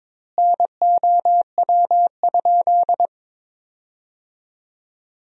Morse code, heard as a keyed tone, about 22 wpm, 700 Hz. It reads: NOW?
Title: 2x Word-Spacing - Sentences from Top 400 Words - 22wpm